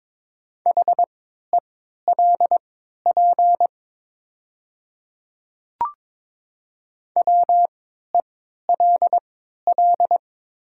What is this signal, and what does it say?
Morse code, heard as a keyed tone, about 23 wpm, spaced as for 13 wpm, 700 Hz. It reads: HELP WELL
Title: 3x Character Spacing - Top 200 Words - 22wpm